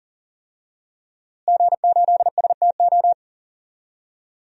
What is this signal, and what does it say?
Morse code, heard as a keyed tone, about 40 wpm, 700 Hz. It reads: G8STO
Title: DXCC-100 Call Signs - 5 Characters - Repeated 3x - 40wpm